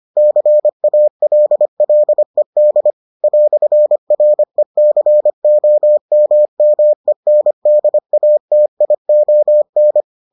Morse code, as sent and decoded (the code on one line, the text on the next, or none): -.-. .- .-.. .-.. -.. / .-..-. .-. -.-. --- -- -- . -. -.. .- - .. --- -.
CALLED "RECOMMENDATION